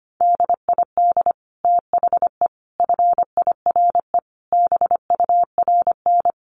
Morse code, heard as a keyed tone, about 25 wpm, 700 Hz. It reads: DID THE FIRE BURN
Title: Sentences from Top 1300 Words 25wpm